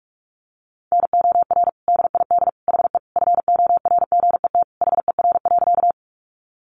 Morse code, heard as a keyed tone, about 44 wpm, 700 Hz. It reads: NOR DID HE FORGET HER.
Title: Sentences from Top 1500 Words 45wpm